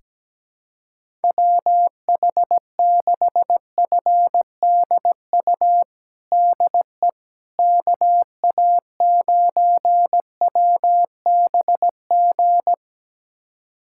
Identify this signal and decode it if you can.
Morse code, heard as a keyed tone, about 17 wpm, 700 Hz. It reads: WH6FDU DE KA9WBG
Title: USA Callsign DE Callsign - 6 Characters - 17wpm